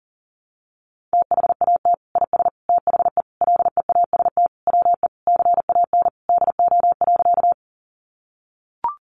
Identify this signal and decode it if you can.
Morse code, heard as a keyed tone, about 40 wpm, 700 Hz. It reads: THAT IS THE LEAST WE CAN DO.
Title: Sentences from Top 700 Words 40wpm